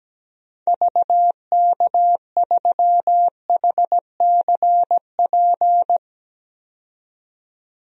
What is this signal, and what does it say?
Morse code, heard as a keyed tone, about 17 wpm, 700 Hz. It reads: VK3HCP
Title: DXCC-100 Call Signs - 6 Characters - 17wpm